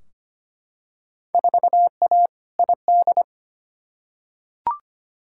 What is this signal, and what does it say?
Morse code, heard as a keyed tone, about 25 wpm, 700 Hz. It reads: 4A ID
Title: Field Day Exchanges - 25wpm